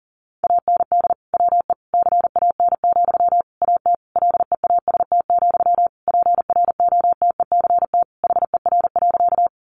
Morse code, heard as a keyed tone, about 40 wpm, 700 Hz. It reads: AND WE CAN, AT LEAST, PROTECT HER.
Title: Sentences from Top 900 Words 40wpm